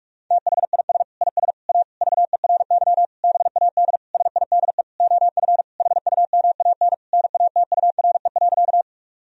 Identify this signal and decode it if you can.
Morse code, heard as a keyed tone, about 46 wpm, 700 Hz. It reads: THIS IS A VERY BAD SIDE OF HUMAN NATURE.
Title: Sentences from Top 1000 Words - Review Part 3 - 45wpm